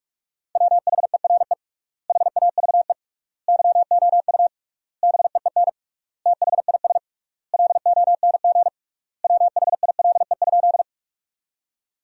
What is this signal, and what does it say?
Morse code, heard as a keyed tone, about 46 wpm, 700 Hz. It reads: WHERE HAVE YOU BEEN THIS LONG WHILE?
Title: 3x Word-Spacing - Sentences from Top 300 Words - 45wpm